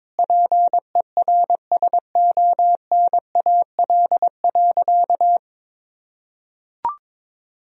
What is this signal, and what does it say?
Morse code, heard as a keyed tone, about 21 wpm, 700 Hz. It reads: PERSONAL.
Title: Sentences from Top 1000 Words - Review Part 4 - 22wpm